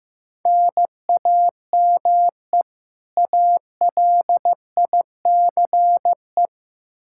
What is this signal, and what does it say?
Morse code, heard as a keyed tone, about 15 wpm, 700 Hz. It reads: NAME ALICE
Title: Practice QSO - Exchange 1 - Beginning - 15wpm